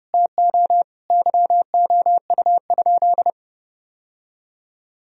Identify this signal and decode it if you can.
Morse code, heard as a keyed tone, about 30 wpm, 700 Hz. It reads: TO YOU?
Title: Sentences from Top 1000 Words - Review Part 1 - 30wpm